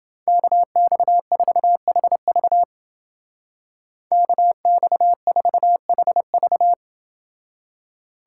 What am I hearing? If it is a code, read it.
Morse code, heard as a keyed tone, about 29 wpm, 700 Hz. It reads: KX4HV KX4HV